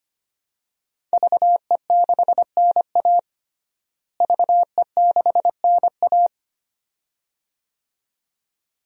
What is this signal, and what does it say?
Morse code, heard as a keyed tone, about 25 wpm, 700 Hz. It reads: VE6NA VE6NA
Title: DXCC-100 Call Signs - 5 Characters - Repeated 2x - 25wpm